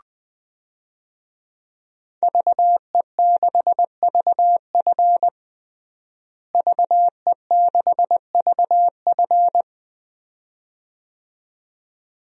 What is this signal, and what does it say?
Morse code, heard as a keyed tone, about 21 wpm, 700 Hz. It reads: VE6VF VE6VF
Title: DXCC-100 Call Signs - 5 Characters - Repeated 2x - 20wpm